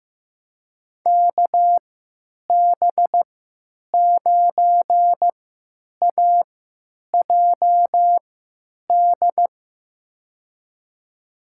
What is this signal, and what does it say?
Morse code, heard as a keyed tone, about 15 wpm, spaced as for 9 wpm, 700 Hz. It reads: KB9AJD
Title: USA Call Signs - 6 Characters - 3x Character Spacing - 15wpm